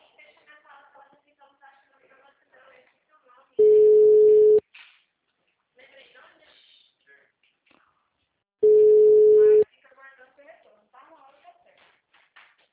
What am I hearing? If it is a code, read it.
Ringback tone (Europe, 425 Hz), with speech.
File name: out- -2020- - - .21760